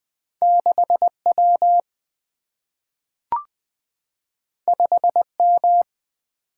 Morse code, heard as a keyed tone, about 20 wpm, 700 Hz. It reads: 6 W 5 M
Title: Single-Letter Single-Number 20wpm